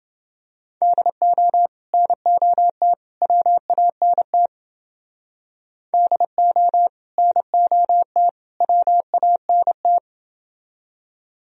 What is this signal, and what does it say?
Morse code, heard as a keyed tone, about 29 wpm, 700 Hz. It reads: DO NOT WANT DO NOT WANT